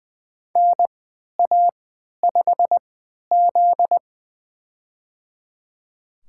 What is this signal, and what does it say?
Morse code, heard as a keyed tone, about 20 wpm, spaced as for 11 wpm, 700 Hz. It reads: NA5Z
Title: USA Call Signs - 4 Characters - 3x Character Spacing - 20wpm